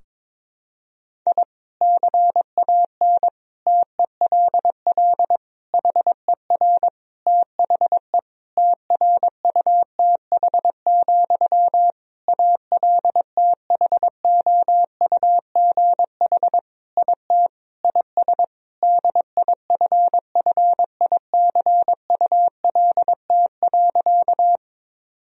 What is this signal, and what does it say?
Morse code, heard as a keyed tone, about 22 wpm, 700 Hz. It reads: I CAN TELL HER THE TRUTH, ALTHOUGH IT IS DIFFICULT.
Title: Sentences from Top 1000 Words - Review Part 3 - 22wpm